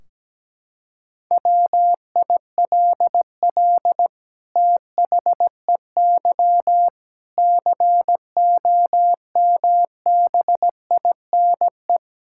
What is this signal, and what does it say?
Morse code, heard as a keyed tone, about 17 wpm, 700 Hz. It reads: WILL THEY COMBINE
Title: Sentences from Top 1600 Words 17wpm